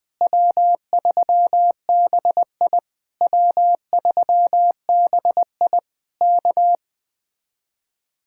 Morse code, heard as a keyed tone, about 20 wpm, 700 Hz. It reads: W3BI W3BI K